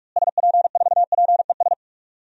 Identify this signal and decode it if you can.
Morse code, heard as a keyed tone, about 44 wpm, 700 Hz. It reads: SP4PES
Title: DXCC-100 Call Signs - 6 Characters - Repeated 2x - 45wpm